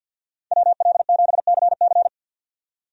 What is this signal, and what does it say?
Morse code, heard as a keyed tone, about 51 wpm, 700 Hz. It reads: WL7CC
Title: USA Call Signs - 5 Characters - 50wpm